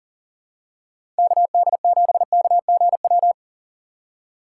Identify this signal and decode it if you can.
Morse code, heard as a keyed tone, about 38 wpm, 700 Hz. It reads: KD7KGW
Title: USA Call Signs - 6 Characters - 3x Character Spacing - 40wpm